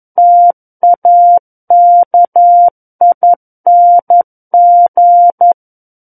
Morse code, heard as a keyed tone, about 11 wpm, 700 Hz. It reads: TAKING
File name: Book One / 02-Chapter 2 - 11 WPM